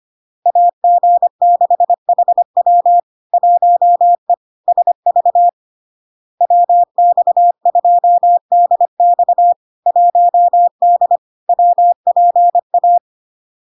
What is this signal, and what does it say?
Morse code, heard as a keyed tone, about 25 wpm, 700 Hz. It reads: AG6HW 1E SV WX2DX 1D WPA
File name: Log-in-CW-25wpm-700-Hz